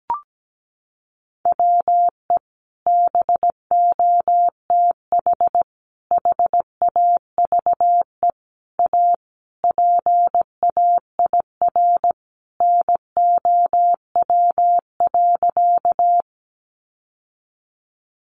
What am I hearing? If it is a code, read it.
Morse code, heard as a keyed tone, about 17 wpm, 700 Hz. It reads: WE BOTH HAVE A PAIR NOW.
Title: Sentences from Top 1200 Words 17wpm